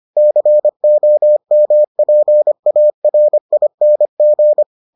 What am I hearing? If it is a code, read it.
Morse code, heard as a keyed tone, about 25 wpm, 600 Hz. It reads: COMPARING